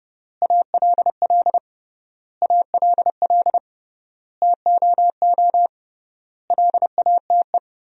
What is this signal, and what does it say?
Morse code, heard as a keyed tone, about 30 wpm, 700 Hz. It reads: ALL ALL TOO LATE
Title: 3x Word-Spacing - Sentences from Top 300 Words - 30wpm